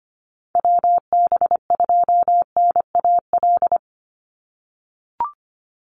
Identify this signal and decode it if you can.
Morse code, heard as a keyed tone, about 25 wpm, 700 Hz. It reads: WB2NAL
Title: USA Call Signs 25wpm